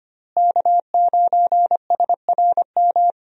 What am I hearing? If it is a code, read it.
Morse code, heard as a keyed tone, about 25 wpm, 700 Hz. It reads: K9SRM
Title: USA Call Signs - 5 Characters - Repeated 2x - 25wpm